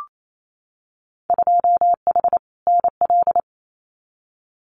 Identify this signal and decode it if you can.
Morse code, heard as a keyed tone, about 28 wpm, 700 Hz. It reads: 2H NL
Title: Winter Field Day Exchanges Encore 28wpm